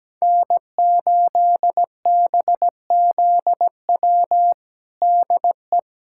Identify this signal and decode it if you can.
Morse code, heard as a keyed tone, about 17 wpm, 700 Hz. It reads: N8BZW DE
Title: USA Callsign DE Callsign 17wpm